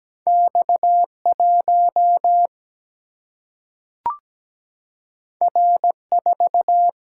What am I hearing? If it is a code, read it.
Morse code, heard as a keyed tone, about 17 wpm, 700 Hz. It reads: X1 R4
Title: Single-Letter Single-Number 17wpm